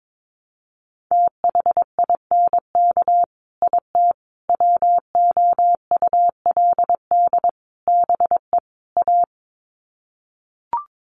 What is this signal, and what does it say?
Morse code, heard as a keyed tone, about 22 wpm, 700 Hz. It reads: THINK IT WOULD BE A
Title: Sets of 5 Words Encore2 22wpm